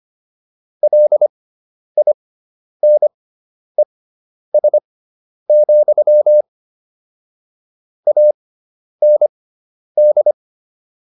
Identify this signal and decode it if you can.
Morse code, heard as a keyed tone, about 25 wpm, spaced as for 10 wpm, 600 Hz. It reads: LINES, AND